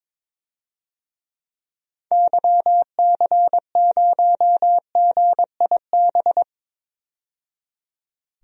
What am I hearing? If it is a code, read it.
Morse code, heard as a keyed tone, about 21 wpm, 700 Hz. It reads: YC0GIB
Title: DXCC-100 Call Signs - 6 Characters - 22wpm